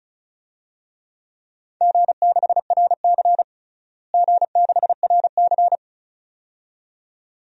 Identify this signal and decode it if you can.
Morse code, heard as a keyed tone, about 36 wpm, 690 Hz. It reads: G6RC G6RC
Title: DXCC-100 Call Signs - 4 Characters - Repeated 2x - 35wpm